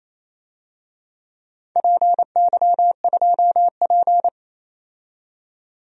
Morse code, heard as a keyed tone, about 28 wpm, 700 Hz. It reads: PY2P
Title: DXCC-100 Call Signs - 4 Characters - 28wpm